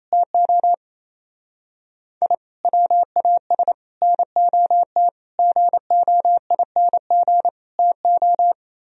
Morse code, heard as a keyed tone, about 32 wpm, 700 Hz. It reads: TO I WAS NOT GOING TO
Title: Speed-Racing - Sets of 5 Words 28wpm